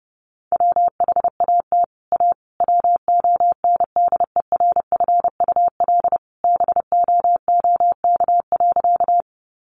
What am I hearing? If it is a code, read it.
Morse code, heard as a keyed tone, about 30 wpm, 700 Hz. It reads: WHAT A WONDERFUL BOOK.
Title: Sentences from Top 1100 Words 30wpm